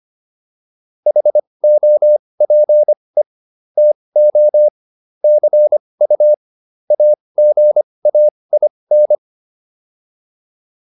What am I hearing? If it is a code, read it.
Morse code, heard as a keyed tone, about 25 wpm, spaced as for 20 wpm, 600 Hz. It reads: HOPE TO CU AGAIN